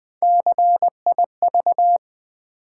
Morse code, heard as a keyed tone, about 20 wpm, 700 Hz. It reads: CIV
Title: Build A Word - 1 Letter At A Time - 5-Letter Words - 20wpm